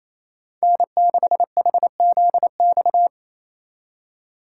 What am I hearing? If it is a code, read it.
Morse code, heard as a keyed tone, about 34 wpm, 700 Hz. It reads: N6HZX
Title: Speed Racing - USA 1x3 Call Signs 28wpm